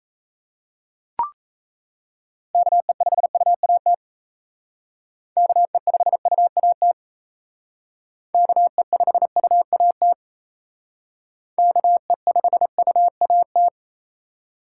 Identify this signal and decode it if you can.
Morse code, heard as a keyed tone, about 37 wpm, 700 Hz. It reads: KE5UAT KE5UAT KE5UAT KE5UAT